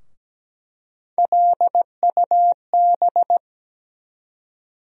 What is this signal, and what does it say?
Morse code, heard as a keyed tone, about 17 wpm, 700 Hz. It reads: LUB
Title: USA Call Sign Postfixes - 3 Letters 17wpm